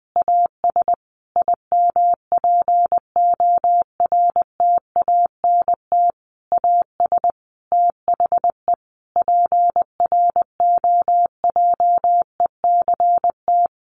Morse code, heard as a keyed tone, about 20 wpm, 700 Hz. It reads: AS IMPORTANT AS THE PROJECT
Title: Sentences from Top 1400 Words 20wpm